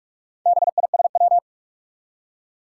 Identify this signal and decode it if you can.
Morse code, heard as a keyed tone, about 44 wpm, 700 Hz. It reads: BISW
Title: Four Challenging Letter-Number Combinations 45wpm